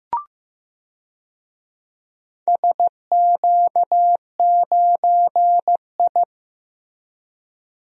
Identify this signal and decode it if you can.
Morse code, heard as a keyed tone, about 16 wpm, 700 Hz. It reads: SQ9I
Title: DXCC-100 Call Signs - 4 Characters - Repeated 2x - 15wpm